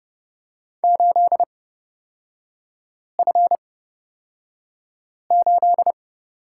Morse code, heard as a keyed tone, about 30 wpm, spaced as for 5 wpm, 690 Hz. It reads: 8F8
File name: Lesson 37e - Letter x - Three Characters 30wpm